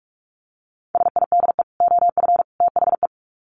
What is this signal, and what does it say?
Morse code, heard as a keyed tone, about 44 wpm, 700 Hz. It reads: SIDE OF THE
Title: Sets of 3 Words 45wpm